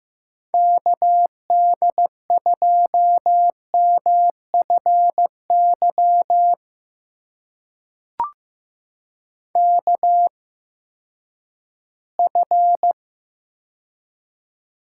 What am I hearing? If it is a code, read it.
Morse code, heard as a keyed tone, about 15 wpm, spaced as for 10 wpm, 700 Hz. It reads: KD2MFY KF
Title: USA Call Signs - 6 Characters - 8x Character Spacing - 15wpm